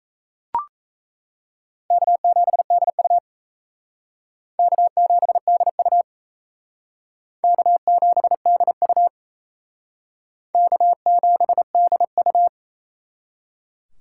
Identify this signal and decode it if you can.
Morse code, heard as a keyed tone, about 35 wpm, 700 Hz. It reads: K7DU K7DU K7DU K7DU
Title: Speed Racing - USA Call Signs 28wpm